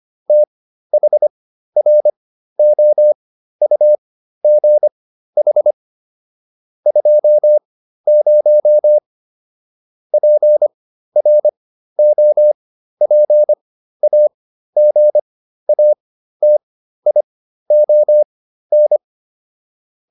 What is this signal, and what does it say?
Morse code, heard as a keyed tone, about 25 wpm, spaced as for 13 wpm, 600 Hz. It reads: THROUGH 20 PROPAGATION
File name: SS211 13